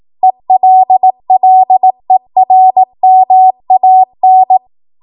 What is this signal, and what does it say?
Morse code, heard as a keyed tone, about 18 wpm, 750 Hz. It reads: ELLERMAN
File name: Morse / 18WPM